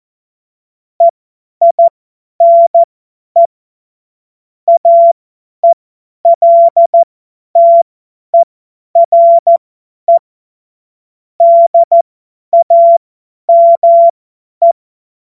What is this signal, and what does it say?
Morse code, heard as a keyed tone, about 14 wpm, spaced as for 10 wpm, 680 Hz. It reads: EINE AELTERE DAME